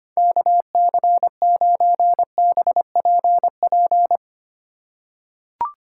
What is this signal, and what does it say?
Morse code, heard as a keyed tone, about 25 wpm, 700 Hz. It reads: KC9BPP
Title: USA Call Signs - 6 Characters - 4x Character Spacing - 25wpm